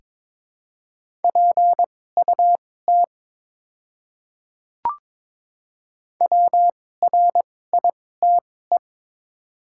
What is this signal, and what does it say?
Morse code, heard as a keyed tone, about 22 wpm, spaced as for 16 wpm, 700 Hz. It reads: PUT WRITE
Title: 2x Character Spacing - Top 500 Words - 22wpm